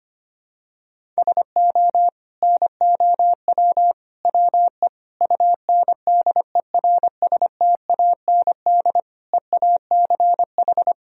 Morse code, heard as a keyed tone, about 25 wpm, 700 Hz. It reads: SO NOW WE UNDERSTAND EACH
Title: Sentences from Top 1000 Words - Review Part 1 - 25wpm